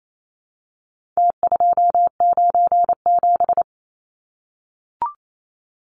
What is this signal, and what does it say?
Morse code, heard as a keyed tone, about 28 wpm, 700 Hz. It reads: T297
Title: Numbers - Contest Style - 4-Digit 28wpm